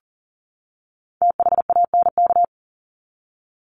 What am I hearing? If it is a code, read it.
Morse code, heard as a keyed tone, about 40 wpm, 700 Hz. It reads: THANK
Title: Top 501-750 Words 40wpm